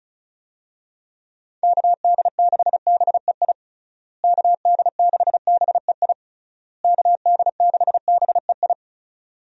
Morse code, heard as a keyed tone, about 34 wpm, 700 Hz. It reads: KD6BEI KD6BEI KD6BEI
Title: USA Call Signs - 6 Characters - Repeated 3x - 35wpm